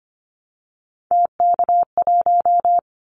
Morse code, heard as a keyed tone, about 25 wpm, 700 Hz. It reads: TK1
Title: Callsign Prefixes for Top 100 DXCC - 25wpm